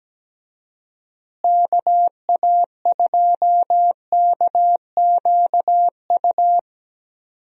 Morse code, heard as a keyed tone, about 17 wpm, 700 Hz. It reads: KA2KQU
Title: USA Call Signs - 6 Characters - Repeated 2x - 17wpm